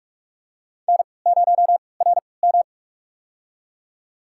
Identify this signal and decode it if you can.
Morse code, heard as a keyed tone, about 43 wpm, spaced as for 25 wpm, 700 Hz. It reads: N0RM
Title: USA Call Signs - 3x Character Spacing - 45wpm